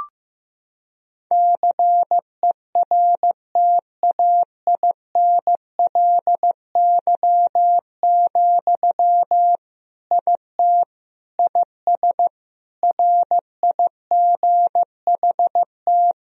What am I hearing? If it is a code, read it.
Morse code, heard as a keyed tone, about 15 wpm, 700 Hz. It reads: CERTAINLY, IT IS RIGHT
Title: Sentences from Top 1000 Words - Review Part 2 - 15wpm